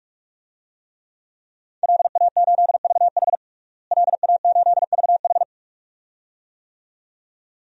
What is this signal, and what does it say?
Morse code, heard as a keyed tone, about 46 wpm, 700 Hz. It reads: LA8VH LA8VH
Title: DXCC-100 Call Signs - 5 Characters - Repeated 2x - 45wpm